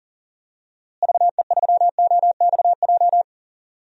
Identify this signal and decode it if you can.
Morse code, heard as a keyed tone, about 37 wpm, 700 Hz. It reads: VE3OXJ
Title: DXCC-100 Call Signs - 6 Characters - Repeated 3x - 40wpm